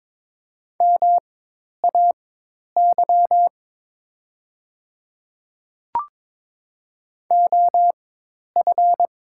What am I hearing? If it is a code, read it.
Morse code, heard as a keyed tone, about 21 wpm, spaced as for 10 wpm, 700 Hz. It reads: MAY OF
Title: 4x Character Spacing - Top 100 Words - 22wpm